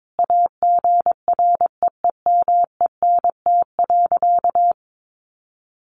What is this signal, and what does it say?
Morse code, heard as a keyed tone, about 22 wpm, 700 Hz. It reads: AGREEMENT.
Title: Sentences from Top 1500 Words 22wpm